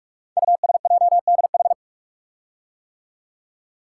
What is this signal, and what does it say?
Morse code, heard as a keyed tone, about 44 wpm, 700 Hz. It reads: USJDH
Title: Code Groups - 5 Letters - 45wpm